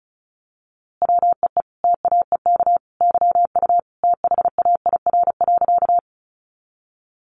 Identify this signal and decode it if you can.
Morse code, heard as a keyed tone, about 34 wpm, 700 Hz. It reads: WEE TAEK YU THAIR.
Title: A Sound Of Thunder 35wpm